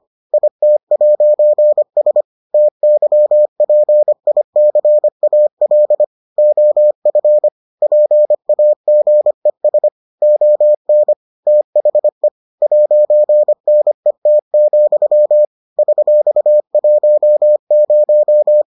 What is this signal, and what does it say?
Morse code, heard as a keyed tone, about 25 wpm, 600 Hz. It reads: IT'S TYPICAL OF PAGES ON THE 'NET, $10